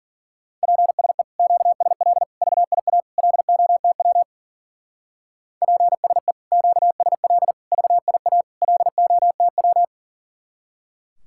Morse code, heard as a keyed tone, about 50 wpm, 700 Hz. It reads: PSE QSL VIA LOTW PSE QSL VIA LOTW
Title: Speed-Racing Common QSO Phrases - 40wpm